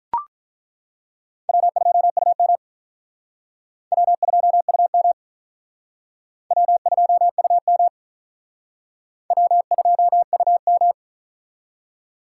Morse code, heard as a keyed tone, about 42 wpm, 700 Hz. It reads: W2UM W2UM W2UM W2UM
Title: Speed Racing - USA 1x2 Call Signs 35wpm